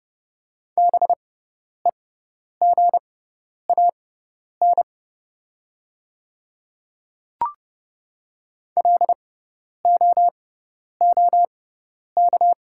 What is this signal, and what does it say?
Morse code, heard as a keyed tone, about 31 wpm, spaced as for 10 wpm, 690 Hz. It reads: BEGAN LOOK
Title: 6x Character Spacing - Top 300 Words - 30wpm